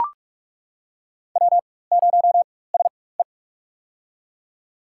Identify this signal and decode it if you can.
Morse code, heard as a keyed tone, about 41 wpm, spaced as for 20 wpm, 700 Hz. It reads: W0SE
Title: USA Call Signs - 4 Characters - 4x Character Spacing - 45wpm